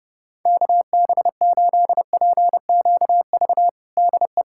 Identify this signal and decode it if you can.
Morse code, heard as a keyed tone, about 32 wpm, 700 Hz. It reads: KB8PQV DE
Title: USA Callsign DE Callsign - 6 Characters - 30wpm